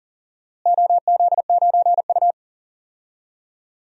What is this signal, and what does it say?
Morse code, heard as a keyed tone, about 40 wpm, 700 Hz. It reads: OZ9U
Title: DXCC-100 Call Signs - 4 Characters - Repeated 2x - 40wpm